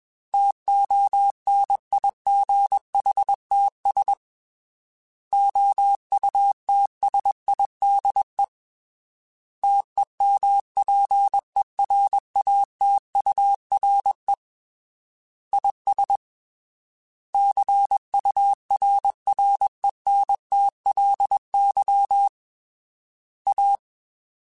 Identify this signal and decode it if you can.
Morse code, heard as a keyed tone, about 21 wpm, 800 Hz. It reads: TONIGHTS OUTSIDE TEMPERATURE IS CURRENTLY A
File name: Weather Sets